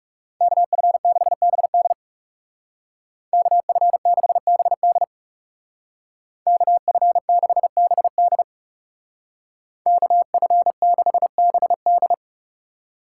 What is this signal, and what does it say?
Morse code, heard as a keyed tone, about 36 wpm, 700 Hz. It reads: KF6BD KF6BD KF6BD KF6BD
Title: Speed Racing - USA 2x2 Call Signs 30wpm